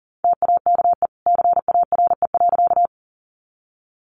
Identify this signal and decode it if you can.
Morse code, heard as a keyed tone, about 39 wpm, 700 Hz. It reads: TAKE CARE.
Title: Sentences from Top 300 Words 40wpm